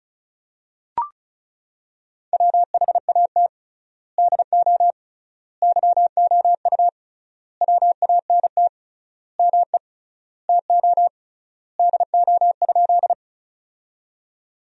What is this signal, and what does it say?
Morse code, heard as a keyed tone, about 34 wpm, 700 Hz. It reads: WHAT DO YOU WANT ME TO DO?
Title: 3x Word-Spacing - Sentences from Top 200 Words - 35wpm